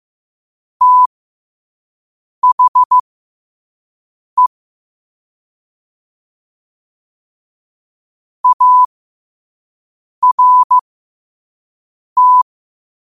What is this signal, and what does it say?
Morse code, heard as a keyed tone, about 15 wpm, spaced as for 5 wpm, 1000 Hz. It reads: THE ART
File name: qotdfiles / 5wpm